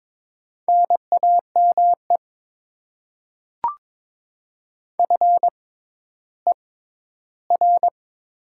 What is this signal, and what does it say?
Morse code, heard as a keyed tone, about 22 wpm, spaced as for 12 wpm, 700 Hz. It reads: NAME FER
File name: Top 20 QSO Elements - 6x Character Spacing - 22wpm